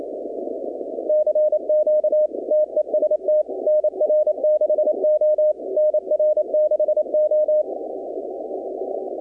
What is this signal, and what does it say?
Morse code, heard as a keyed tone, about 27 wpm, 620 Hz. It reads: CQ TEST NR6O NR6O
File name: contests / 2019-cw-wpx-kt5j-k5pi- -2330 0-20